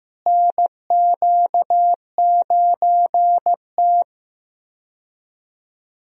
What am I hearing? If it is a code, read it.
Morse code, heard as a keyed tone, about 15 wpm, 700 Hz. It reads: NQ9T
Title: USA Call Signs - 4 Characters - Repeated 2x - 15wpm